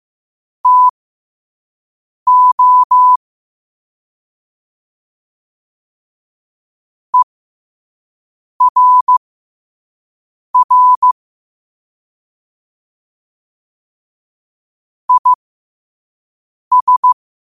Morse code, heard as a keyed tone, about 15 wpm, spaced as for 5 wpm, 1000 Hz. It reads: TO ERR IS
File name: qotdfiles / 5wpm